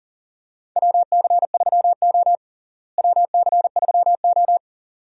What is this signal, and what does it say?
Morse code, heard as a keyed tone, about 40 wpm, 700 Hz. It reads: WC3O WC3O
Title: USA Call Signs - 4 Characters - Repeated 2x - 40wpm